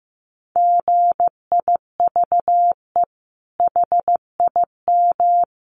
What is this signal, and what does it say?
Morse code, heard as a keyed tone, about 15 wpm, 700 Hz. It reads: GIVE HIM
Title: Sentences from Top 600 Words 15wpm